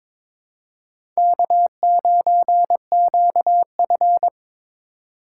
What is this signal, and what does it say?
Morse code, heard as a keyed tone, about 22 wpm, 700 Hz. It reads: K9QF
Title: USA Call Signs - 2x Character Spacing - 22wpm